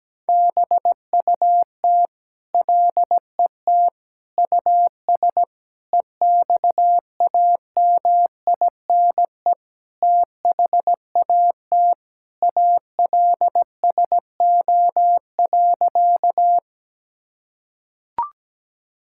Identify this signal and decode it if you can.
Morse code, heard as a keyed tone, about 17 wpm, 700 Hz. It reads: BUT LET US EXAMINE THAT ALSO.
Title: Sentences from Top 1600 Words 17wpm